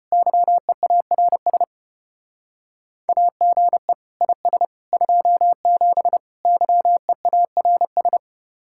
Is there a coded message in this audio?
Yes: Morse code from 0.1 to 8.2 s, heard as a keyed tone, about 33 wpm, 700 Hz. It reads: YEARS AGE IS 27 YEARS